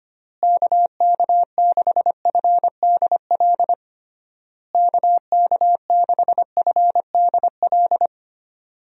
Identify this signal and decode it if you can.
Morse code, heard as a keyed tone, about 25 wpm, 700 Hz. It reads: KK6FDL KK6FDL
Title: USA Call Signs - 6 Characters - Repeated 3x - 25wpm